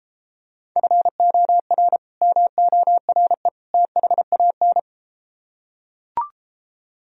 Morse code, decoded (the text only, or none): FOR MORE THAN